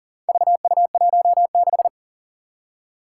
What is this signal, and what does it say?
Morse code, heard as a keyed tone, about 40 wpm, 700 Hz. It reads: VU16
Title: Four Challenging Letter-Number Combinations 40wpm